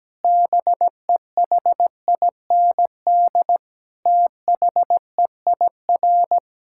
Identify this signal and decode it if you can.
Morse code, heard as a keyed tone, about 17 wpm, 700 Hz. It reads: BEHIND THEIR
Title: Sentences from Top 1000 Words - Review Part 5 - 17wpm